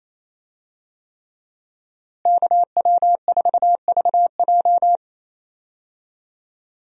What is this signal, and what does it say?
Morse code, heard as a keyed tone, about 27 wpm, 700 Hz. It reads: KW4VJ
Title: USA Call Signs - 5 Characters - Repeated 3x - 28wpm